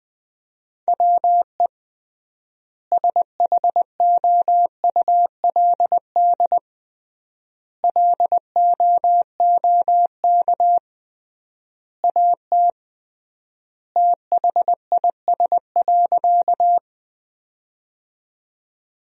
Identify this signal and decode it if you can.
Morse code, heard as a keyed tone, about 20 wpm, 700 Hz. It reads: WE SHOULD LOOK AT THIS.
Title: 3x Word-Spacing - Sentences from Top 300 Words - 20wpm